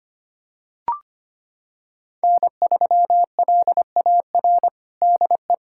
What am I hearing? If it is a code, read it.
Morse code, heard as a keyed tone, about 25 wpm, 700 Hz. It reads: N3LAR DE
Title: USA Callsign DE Callsign - 5 Characters - 25wpm